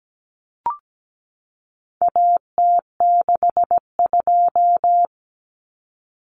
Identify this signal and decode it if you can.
Morse code, heard as a keyed tone, about 17 wpm, 700 Hz. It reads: AT62
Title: Common Cut Numbers - 4-Digit 17wpm